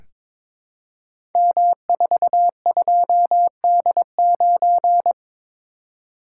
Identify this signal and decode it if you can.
Morse code, heard as a keyed tone, about 22 wpm, 700 Hz. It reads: M42D9
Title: Code Groups - 5 Letter:Numbers - 22wpm